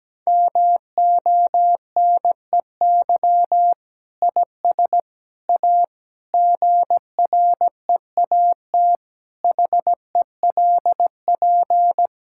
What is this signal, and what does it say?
Morse code, heard as a keyed tone, about 17 wpm, 700 Hz. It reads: MONEY IS A GREAT HELP
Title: Sentences from Top 1600 Words 17wpm